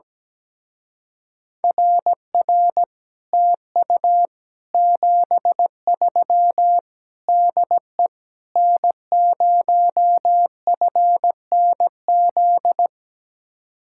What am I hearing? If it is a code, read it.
Morse code, heard as a keyed tone, about 17 wpm, 700 Hz. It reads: RR TU 73 DE N0FNZ
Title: Speed-Racing Common QSO Phrases Encore - 17wpm